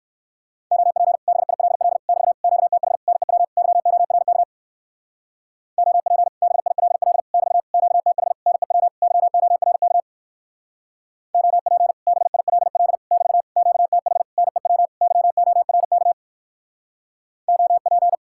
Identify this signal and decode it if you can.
Morse code, heard as a keyed tone, about 57 wpm, 700 Hz. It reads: OP BILL <BT> QTH NEW YORK OP BILL <BT> QTH NEW YORK OP BILL <BT> QTH NEW YORK OP